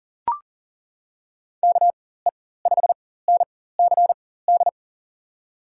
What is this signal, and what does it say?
Morse code, heard as a keyed tone, about 40 wpm, spaced as for 19 wpm, 700 Hz. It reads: KE5NCD
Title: USA Call Signs - 6 Characters - 4x Character Spacing - 40wpm